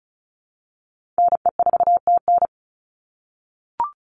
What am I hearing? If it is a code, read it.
Morse code, heard as a keyed tone, about 34 wpm, 700 Hz. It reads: NE4TN
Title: NA SOTA Chaser Callsigns 35wpm